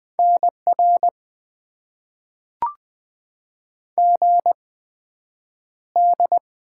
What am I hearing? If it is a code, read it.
Morse code, heard as a keyed tone, about 20 wpm, spaced as for 9 wpm, 700 Hz. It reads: NR GD
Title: Top 90 QSO Elements - 8x Character Spacing - 20wpm